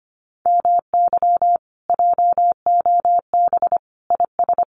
Morse code, heard as a keyed tone, about 25 wpm, 700 Hz. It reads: MY JOB IS
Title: Common QSO Phrases Encore 25wpm